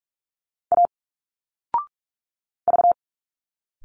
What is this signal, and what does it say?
Morse code, heard as a keyed tone, about 46 wpm, 700 Hz. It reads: A V
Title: Single Letters 45wpm